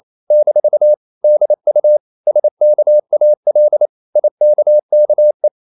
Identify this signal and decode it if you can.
Morse code, heard as a keyed tone, about 28 wpm, 600 Hz. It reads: - DU SKAL IKKE
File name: Ferie QRQ 28wpm